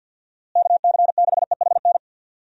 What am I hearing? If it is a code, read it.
Morse code, heard as a keyed tone, about 48 wpm, 700 Hz. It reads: KC6EHN
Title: USA Call Signs - 6 Characters - Repeated 3x - 50wpm